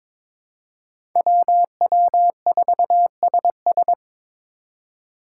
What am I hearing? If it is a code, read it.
Morse code, heard as a keyed tone, about 26 wpm, 700 Hz. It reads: WW4SS
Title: Speed Racing - USA 2x2 Call Signs 22wpm